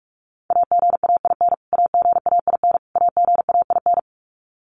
Morse code, heard as a keyed tone, about 44 wpm, 700 Hz. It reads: AGAIN AGAIN AGAIN
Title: Top 200 Words Repeated 45wpm